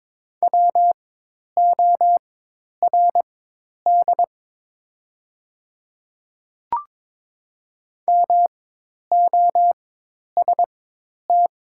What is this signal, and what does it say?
Morse code, heard as a keyed tone, about 22 wpm, spaced as for 10 wpm, 700 Hz. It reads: WORD MOST
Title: 4x Character Spacing - Top 100 Words - 22wpm